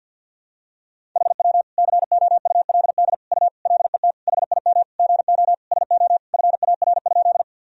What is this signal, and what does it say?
Morse code, heard as a keyed tone, about 46 wpm, 700 Hz. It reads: HW COULD U LET HIM GO SO FAR?
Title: Sentences from Top 300 Words and Abbreviations 50wpm